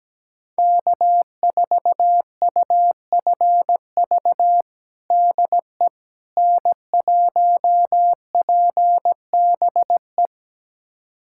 Morse code, heard as a keyed tone, about 17 wpm, 700 Hz. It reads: K4UFV DE N1PBE
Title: USA Callsign DE Callsign - 5 Characters - 17wpm